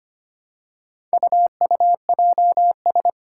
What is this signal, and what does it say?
Morse code, heard as a keyed tone, about 25 wpm, 700 Hz. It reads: UUJS
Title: Four Challenging Letter-Number Combinations 25wpm